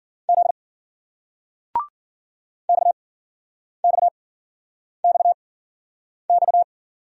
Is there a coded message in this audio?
Yes: Morse code from 0.3 to 6.6 s, heard as a keyed tone, about 56 wpm, spaced as for 9 wpm, 700 Hz. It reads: Z XXXX